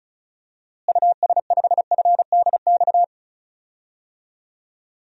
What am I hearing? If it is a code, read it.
Morse code, heard as a keyed tone, about 36 wpm, 700 Hz. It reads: US5FDX
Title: DXCC-100 Call Signs - 6 Characters - Repeated 2x - 35wpm